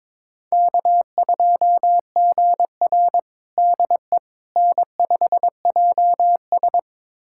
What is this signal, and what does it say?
Morse code, heard as a keyed tone, about 22 wpm, 700 Hz. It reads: K2GR DE N5JS
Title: USA Callsign DE Callsign - 4 Characters - 22wpm